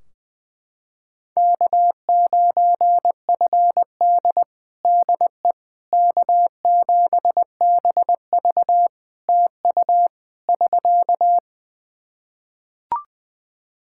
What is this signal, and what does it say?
Morse code, heard as a keyed tone, about 21 wpm, 700 Hz. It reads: K9FD DE K7BV TU <SK>
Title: Practice QSO - Exchange 4 - Ending - 20wpm